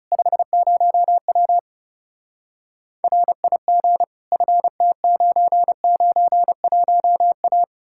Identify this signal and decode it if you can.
Morse code, heard as a keyed tone, about 40 wpm, 700 Hz. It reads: 50W RIG FT991A